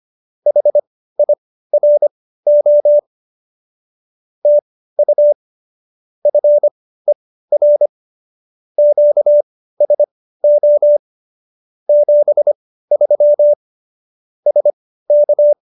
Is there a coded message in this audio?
Yes: Morse code from 0.5 to 15.5 s, heard as a keyed tone, about 25 wpm, spaced as for 15 wpm, 590 Hz. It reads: HIRO TU FER QSO 73 SK